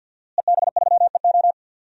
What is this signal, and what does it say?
Morse code, heard as a keyed tone, about 48 wpm, 700 Hz. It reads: EB3EO
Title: DXCC-100 Call Signs - 5 Characters - Repeated 2x - 50wpm